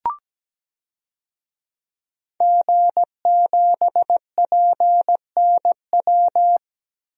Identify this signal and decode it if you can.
Morse code, heard as a keyed tone, about 17 wpm, 700 Hz. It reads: G7PNW
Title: DXCC-100 Call Signs - 5 Characters - 17wpm